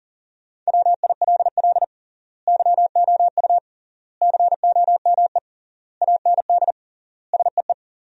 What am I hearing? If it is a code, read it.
Morse code, heard as a keyed tone, about 40 wpm, 700 Hz. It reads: WILL YOU COME AND SEE
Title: 3x Word-Spacing - Sentences from Top 100 Words - 40wpm